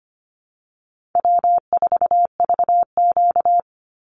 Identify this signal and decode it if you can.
Morse code, heard as a keyed tone, about 25 wpm, 700 Hz. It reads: W4VQ
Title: Call Signs Repeated 2x - 25wpm